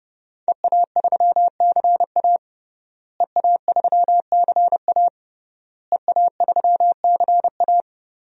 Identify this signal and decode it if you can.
Morse code, heard as a keyed tone, about 31 wpm, 700 Hz. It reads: EA3CA EA3CA EA3CA
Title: DXCC-100 Call Signs - 5 Characters - Repeated 3x - 30wpm